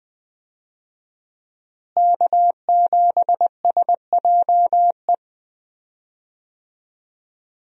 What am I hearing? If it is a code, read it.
Morse code, heard as a keyed tone, about 20 wpm, 700 Hz. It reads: K7SJE
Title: USA Call Signs - 5 Characters - Repeated 2x - 20wpm